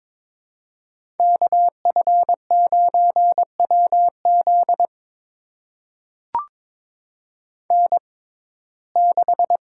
Morse code, heard as a keyed tone, about 22 wpm, spaced as for 16 wpm, 700 Hz. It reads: KF9WZ N6
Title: USA Call Signs - 5 Characters - 6x Character Spacing - 22wpm